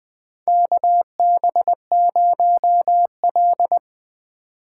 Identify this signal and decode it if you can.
Morse code, heard as a keyed tone, about 20 wpm, 700 Hz. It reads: KB0L
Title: USA Call Signs - 4 Characters - 6x Character Spacing - 20wpm